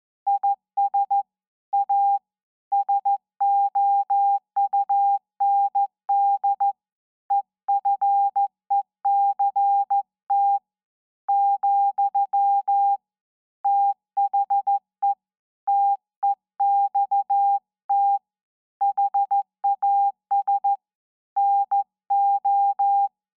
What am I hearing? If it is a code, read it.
Morse code, heard as a keyed tone, about 14 wpm, 800 Hz. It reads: IS A SOUND EFECT , THE TEXT HAS NO
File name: se / pre2 / Morse Code02-2(Slow)